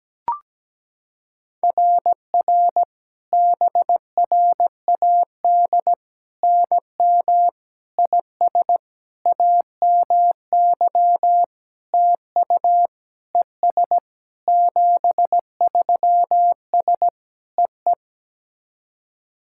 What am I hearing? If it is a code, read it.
Morse code, heard as a keyed tone, about 17 wpm, 700 Hz. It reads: RR BRAD NM IS AMY TU ES 73S EE
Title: SOTA QSO Practice 17wpm